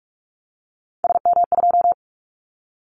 Morse code, heard as a keyed tone, about 46 wpm, 700 Hz. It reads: SM2
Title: Callsign Prefixes for Top 25 DXCC - 45wpm